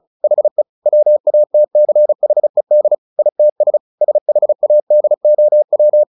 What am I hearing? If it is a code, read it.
Morse code, heard as a keyed tone, about 34 wpm, 600 Hz. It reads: HE WATCHED ITS SHADOW
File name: the old man and the sea 2 35wpm